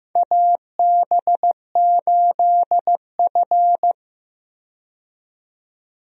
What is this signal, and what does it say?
Morse code, heard as a keyed tone, about 15 wpm, 700 Hz. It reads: AB8F
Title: USA Call Signs - 4 Characters - 15wpm